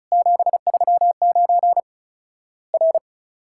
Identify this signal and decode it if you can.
Morse code, heard as a keyed tone, about 34 wpm, 700 Hz. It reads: 739 R
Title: Worked All Europe DX Contest - QTC Exchange - 35wpm